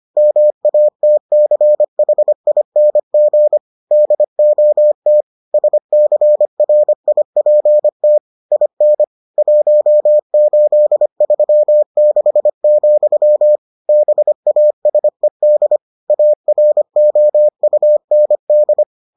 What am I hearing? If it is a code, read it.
Morse code, heard as a keyed tone, about 25 wpm, 600 Hz. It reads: MATCHING DOT SCRIPT IN 1836, BASED AROUND